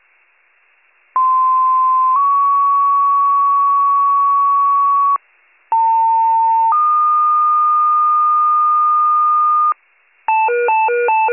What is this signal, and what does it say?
Two-tone page, heard as a one-second tone, then a three-second tone, about 1040 then 1120 Hz, about 900 then 1220 Hz. It is heard from 1.2 to 9.7 s.